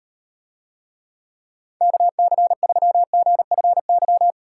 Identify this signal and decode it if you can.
Morse code, heard as a keyed tone, about 38 wpm, 700 Hz. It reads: KC3GFY